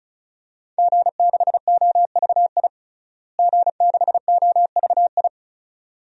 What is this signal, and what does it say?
Morse code, heard as a keyed tone, about 36 wpm, 700 Hz. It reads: G6OVI G6OVI
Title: DXCC-100 Call Signs - 5 Characters - Repeated 2x - 35wpm